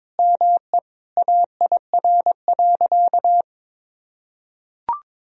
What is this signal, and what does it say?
Morse code, heard as a keyed tone, about 22 wpm, 700 Hz. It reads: ME AIR.
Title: Sentences from Top 1000 Words - Review Part 1 - 22wpm